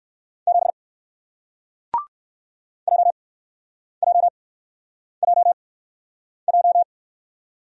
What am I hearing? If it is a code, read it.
Morse code, heard as a keyed tone, about 58 wpm, spaced as for 9 wpm, 700 Hz. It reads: 7 JJJJ